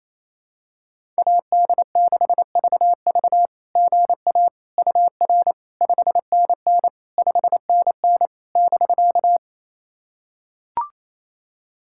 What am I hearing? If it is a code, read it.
Morse code, heard as a keyed tone, about 28 wpm, 690 Hz. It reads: AD6VV GA UR 5NN 5NN <BK>
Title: SOTA QSO Practice 28wpm